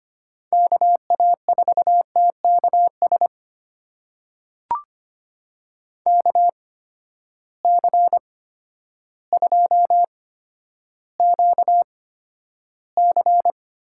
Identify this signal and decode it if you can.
Morse code, heard as a keyed tone, about 25 wpm, spaced as for 11 wpm, 700 Hz. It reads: KA4TKS KC2QC
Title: USA Call Signs - 6 Characters - 8x Character Spacing - 25wpm